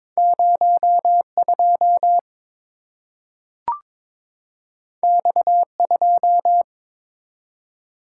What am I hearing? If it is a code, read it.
Morse code, heard as a keyed tone, about 22 wpm, 700 Hz. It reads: 02 X2
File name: practice / 22wpm / Two Letter-Number Combinations - 22wpm